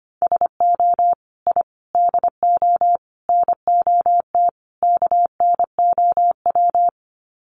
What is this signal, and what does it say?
Morse code, heard as a keyed tone, about 25 wpm, 700 Hz. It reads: SO I DO NOT KNOW
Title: Sets of 5 Words Encore 25wpm